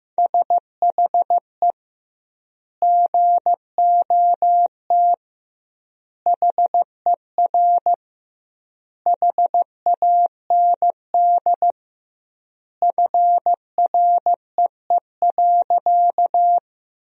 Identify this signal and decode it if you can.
Morse code, heard as a keyed tone, about 15 wpm, 700 Hz. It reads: SHE GOT HER HAND FREE.
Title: 2x Word-Spacing - Sentences from Top 500 Words - 15wpm